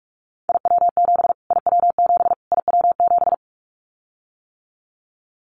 Morse code, heard as a keyed tone, about 44 wpm, 700 Hz. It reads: IW7 IW7 IW7
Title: Callsign Prefixes for Top 25 DXCC - Repeated 3x - 45wpm